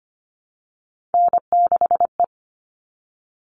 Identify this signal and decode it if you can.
Morse code, heard as a keyed tone, about 25 wpm, 700 Hz. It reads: N6E
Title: All Cut Numbers - 3-Digit 25wpm